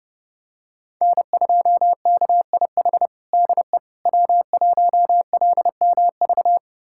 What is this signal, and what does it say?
Morse code, heard as a keyed tone, about 32 wpm, 700 Hz. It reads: N2KIH DE W1LMV
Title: USA Callsign DE Callsign - 5 Characters - 30wpm